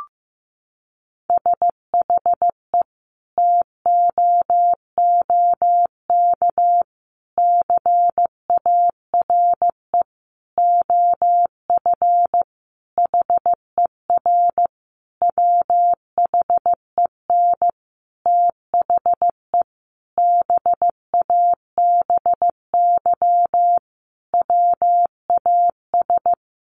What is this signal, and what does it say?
Morse code, heard as a keyed tone, about 15 wpm, 700 Hz. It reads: SHE TOOK CARE OF HER WHEN THE BABY WAS
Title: Sentences from Top 1300 Words 15wpm